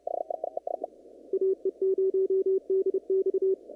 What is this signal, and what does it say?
Morse code, heard as a keyed tone, about 30 wpm, 380 Hz. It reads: AE0DX
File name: CW / kt5j wpx cw-op-k5pi